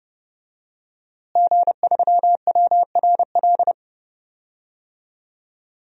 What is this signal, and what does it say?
Morse code, heard as a keyed tone, about 30 wpm, 700 Hz. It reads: G3WRL
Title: DXCC-100 Call Signs - 5 Characters - Repeated 2x - 30wpm